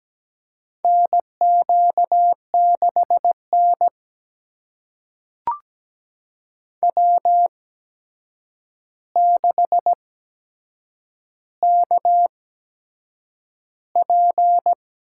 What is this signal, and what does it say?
Morse code, heard as a keyed tone, about 17 wpm, spaced as for 7 wpm, 700 Hz. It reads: NQ6N W6KP
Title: USA Call Signs - 4 Characters - 8x Character Spacing - 17wpm